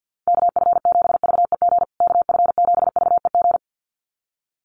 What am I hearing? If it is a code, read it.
Morse code, heard as a keyed tone, about 49 wpm, 700 Hz. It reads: KF7VEG KF7VEG
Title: USA Call Signs Repeated 2x 50wpm